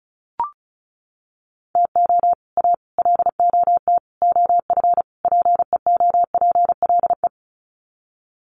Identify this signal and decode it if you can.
Morse code, heard as a keyed tone, about 34 wpm, 700 Hz. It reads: TO A LOT OF PEOPLE
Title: Sets of 5 Words Encore 35wpm